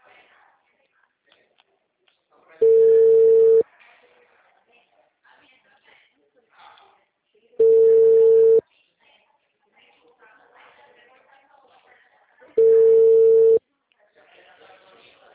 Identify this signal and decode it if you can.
Ringback tone (China, 450 Hz), with speech.